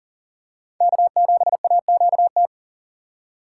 Morse code, heard as a keyed tone, about 37 wpm, 700 Hz. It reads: K7AQT
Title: USA Call Signs - 8x Character Spacing - 40wpm